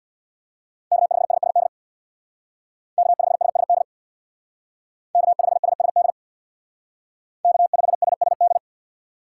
Morse code, heard as a keyed tone, about 58 wpm, 700 Hz. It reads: K5SSD K5SSD K5SSD K5SSD
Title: Speed Racing - USA 1x3 Call Signs 50wpm